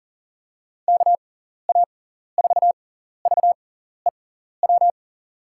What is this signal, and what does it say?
Morse code, heard as a keyed tone, about 40 wpm, spaced as for 14 wpm, 700 Hz. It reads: KA4VEW
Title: USA Call Signs - 6 Characters - 6x Character Spacing - 40wpm